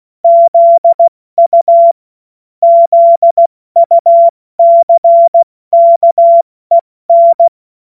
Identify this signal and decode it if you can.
Morse code, heard as a keyed tone, about 16 wpm, spaced as for 14 wpm, 680 Hz. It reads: ZU ZUCKEN
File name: Morsekurs / Witze / Witz 21.November 2023 70 F80